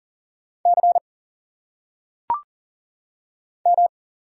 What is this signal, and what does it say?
Morse code, heard as a keyed tone, about 39 wpm, 700 Hz. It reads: C M
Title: Single Letters - Phonetic Pronunciation 40wpm